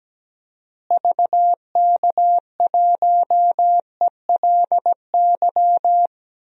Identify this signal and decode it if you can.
Morse code, heard as a keyed tone, about 17 wpm, 700 Hz. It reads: VK1ELY
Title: DXCC-100 Call Signs - 6 Characters - Repeated 3x - 17wpm